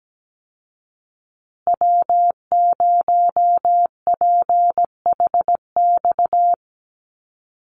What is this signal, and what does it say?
Morse code, heard as a keyed tone, about 17 wpm, 700 Hz. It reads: W0PHX
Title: Call Signs Repeated 3x - 17wpm